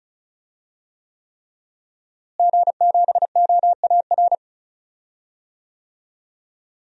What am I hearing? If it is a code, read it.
Morse code, heard as a keyed tone, about 34 wpm, 700 Hz. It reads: G7OAR
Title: DXCC-100 Call Signs - 5 Characters - Repeated 3x - 35wpm